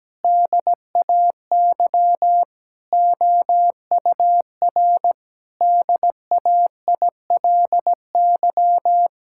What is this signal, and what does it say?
Morse code, heard as a keyed tone, about 17 wpm, 700 Hz. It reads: DAY OUR DAILY
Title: Sentences from Top 1600 Words 17wpm